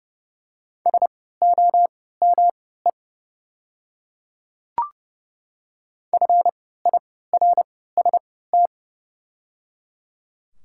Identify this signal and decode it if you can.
Morse code, heard as a keyed tone, about 30 wpm, spaced as for 17 wpm, 700 Hz. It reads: SOME FIRST
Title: 3x Character Spacing - Top 100 Words - 30wpm